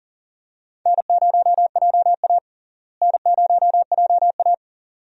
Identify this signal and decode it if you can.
Morse code, heard as a keyed tone, about 40 wpm, 700 Hz. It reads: N0JA N0JA
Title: USA Call Signs - 4 Characters - Repeated 2x - 40wpm